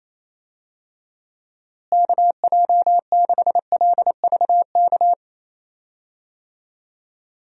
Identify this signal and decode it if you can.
Morse code, heard as a keyed tone, about 28 wpm, 700 Hz. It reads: KJ6LVK
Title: USA Call Signs - 6 Characters - Repeated 3x - 28wpm